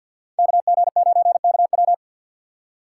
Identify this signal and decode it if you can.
Morse code, heard as a keyed tone, about 53 wpm, 700 Hz. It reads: KG9KW
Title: USA Call Signs - 5 Characters - 3x Character Spacing - 50wpm